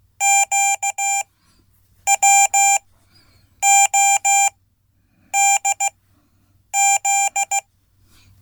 Morse code, heard as a keyed tone, about 16 wpm, spaced as for 8 wpm, 790 Hz. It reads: QWODZ
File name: Code / 5min-b